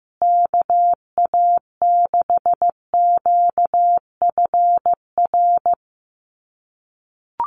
Morse code, heard as a keyed tone, about 15 wpm, 700 Hz. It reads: KA6QFR
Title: USA Call Signs Repeated 3x 15wpm